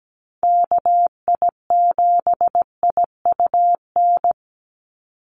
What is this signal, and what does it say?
Morse code, heard as a keyed tone, about 17 wpm, 700 Hz. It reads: KI7IUN
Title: USA Call Signs 17wpm